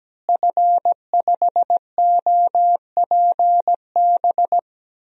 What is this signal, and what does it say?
Morse code, heard as a keyed tone, about 17 wpm, 700 Hz. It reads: F5OPB
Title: DXCC-100 Call Signs - 5 Characters - Repeated 2x - 17wpm